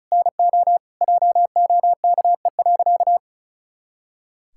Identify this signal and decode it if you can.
Morse code, heard as a keyed tone, about 34 wpm, 700 Hz. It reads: NO JOKE.
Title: Pangram Challenge v2 - 35wpm